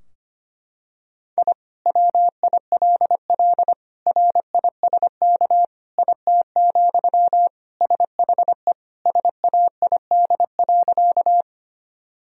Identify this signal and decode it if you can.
Morse code, heard as a keyed tone, about 25 wpm, 700 Hz. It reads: I WILL RISK IT, SHE SAID.
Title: Sentences from Top 1000 Words - Review Part 4 - 25wpm